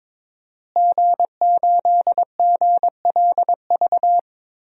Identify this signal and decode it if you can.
Morse code, heard as a keyed tone, about 22 wpm, 700 Hz. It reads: G8GLV
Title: DXCC-100 Call Signs - 5 Characters - Repeated 3x - 22wpm